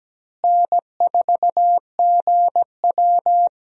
Morse code, heard as a keyed tone, about 17 wpm, 700 Hz. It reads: N4GW
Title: USA Call Signs - 4 Characters - Repeated 2x - 17wpm